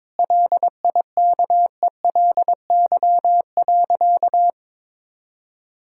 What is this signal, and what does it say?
Morse code, heard as a keyed tone, about 22 wpm, 700 Hz. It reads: LIKELY.
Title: Sentences from Top 1600 Words 22wpm